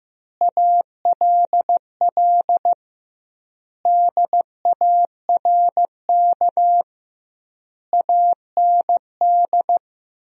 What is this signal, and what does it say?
Morse code, heard as a keyed tone, about 15 wpm, 700 Hz. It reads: ALL DARK AND
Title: 2x Word-Spacing - Sentences from Top 500 Words - 15wpm